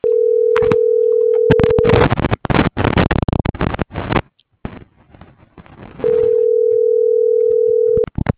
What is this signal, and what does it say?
Ringback tone (North America), 440 + 480 Hz.